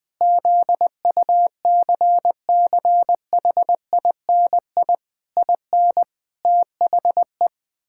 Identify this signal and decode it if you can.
Morse code, heard as a keyed tone, about 21 wpm, 700 Hz. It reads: ZUCCHINI IN THE
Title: Pangram Challenge v2 - 20wpm